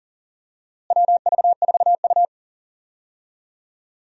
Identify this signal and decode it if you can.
Morse code, heard as a keyed tone, about 40 wpm, 700 Hz. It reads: WV4U
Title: USA Call Signs - 4 Characters - Repeated 2x - 40wpm